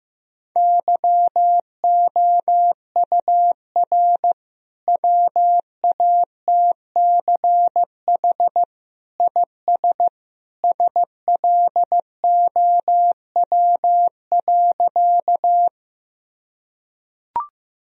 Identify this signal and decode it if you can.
Morse code, heard as a keyed tone, about 15 wpm, 700 Hz. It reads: YOUR WATCH IS SLOW.
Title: Sentences from Top 1000 Words - Review Part 2 - 15wpm